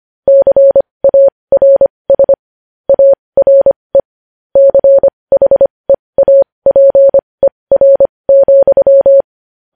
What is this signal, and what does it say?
Morse code, heard as a keyed tone, about 25 wpm, spaced as for 20 wpm, 550 Hz. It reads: CARS ARE CHEAPER,
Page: news-headlines-in-morse-code-at-20-wpm